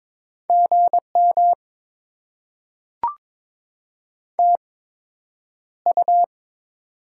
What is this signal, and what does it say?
Morse code, heard as a keyed tone, about 22 wpm, spaced as for 9 wpm, 700 Hz. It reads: GM TU
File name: Top 20 QSO Elements - 8x Character Spacing - 22wpm